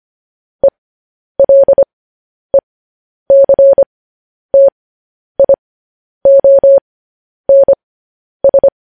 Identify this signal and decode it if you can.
Morse code, heard as a keyed tone, about 25 wpm, spaced as for 10 wpm, 550 Hz. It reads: ELECTIONS
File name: podcast / News headlines-1x-25x10